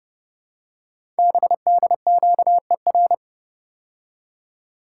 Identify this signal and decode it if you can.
Morse code, heard as a keyed tone, about 30 wpm, 700 Hz. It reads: BDQER
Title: Code Groups - 5 Letters - 30wpm